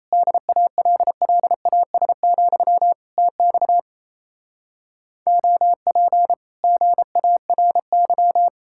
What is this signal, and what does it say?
Morse code, heard as a keyed tone, about 38 wpm, 700 Hz. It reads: DALLAS, TX OP GARY